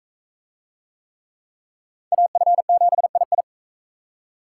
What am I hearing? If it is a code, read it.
Morse code, heard as a keyed tone, about 42 wpm, 700 Hz. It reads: AF7II